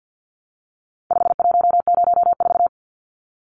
Morse code, heard as a keyed tone, about 52 wpm, 700 Hz. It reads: 5104
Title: Numbers - 4-Digit 50wpm